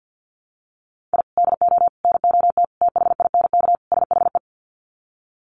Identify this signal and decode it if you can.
Morse code, heard as a keyed tone, about 48 wpm, 700 Hz. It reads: I DO NOT THINK SHE
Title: Sets of 5 Words 50wpm